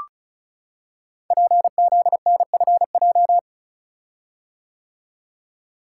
Morse code, heard as a keyed tone, about 34 wpm, 690 Hz. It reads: PZNFJ
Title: Code Groups - 5 Letters - 35wpm